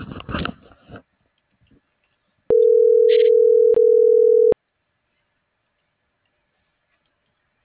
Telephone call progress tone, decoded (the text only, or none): ringback tone